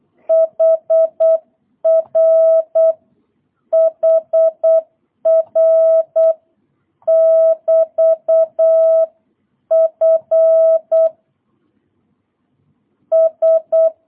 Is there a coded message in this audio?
Yes: Morse code from 0.3 to 13.9 s, heard as a keyed tone, about 8 wpm, 660 Hz. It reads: HR HR <BT> F S